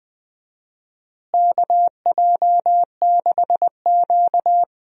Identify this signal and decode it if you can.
Morse code, heard as a keyed tone, about 21 wpm, 700 Hz. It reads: KJ6Q
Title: USA Call Signs - 4 Characters - Repeated 3x - 20wpm